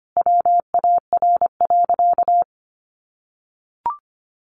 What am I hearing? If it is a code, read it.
Morse code, heard as a keyed tone, about 25 wpm, 700 Hz. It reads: WAR.
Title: Sentences from Top 1400 Words 25wpm